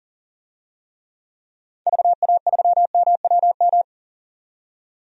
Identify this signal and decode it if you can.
Morse code, heard as a keyed tone, about 40 wpm, 700 Hz. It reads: VA3MWM
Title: DXCC-100 Call Signs - 6 Characters - 40wpm